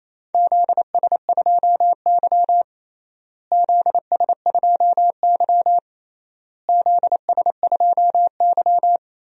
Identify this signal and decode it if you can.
Morse code, heard as a keyed tone, about 28 wpm, 700 Hz. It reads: ZS2Y ZS2Y ZS2Y
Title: DXCC-100 Call Signs - 4 Characters - Repeated 3x - 28wpm